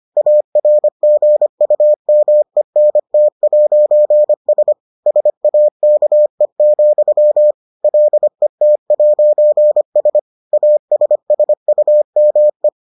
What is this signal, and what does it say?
Morse code, heard as a keyed tone, about 25 wpm, 600 Hz. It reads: ARGUMENT'S SAKE, LET'S ASSUME